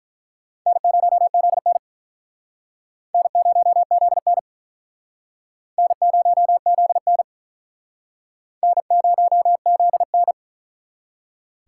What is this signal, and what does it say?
Morse code, heard as a keyed tone, about 48 wpm, 700 Hz. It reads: N0ZN N0ZN N0ZN N0ZN